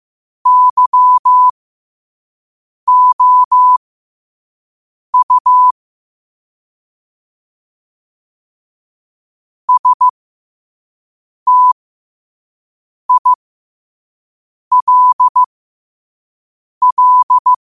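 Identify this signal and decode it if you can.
Morse code, heard as a keyed tone, about 15 wpm, spaced as for 5 wpm, 1000 Hz. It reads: YOU STILL
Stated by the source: Morse quote-of-the-day practice at 5 wpm.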